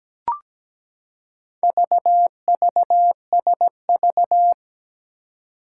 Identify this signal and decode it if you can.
Morse code, heard as a keyed tone, about 17 wpm, 700 Hz. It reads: VVSV
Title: Four Challenging Letter-Number Combinations 17wpm